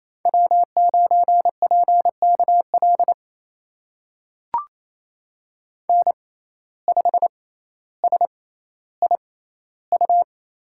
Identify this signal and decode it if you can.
Morse code, heard as a keyed tone, about 28 wpm, spaced as for 14 wpm, 700 Hz. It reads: W9PKL N5SIU
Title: USA Call Signs - 5 Characters - 6x Character Spacing - 28wpm